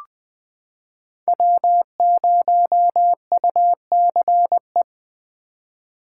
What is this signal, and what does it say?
Morse code, heard as a keyed tone, about 20 wpm, 700 Hz. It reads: W0UCE
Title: K1USN SST QSO Practice - 20wpm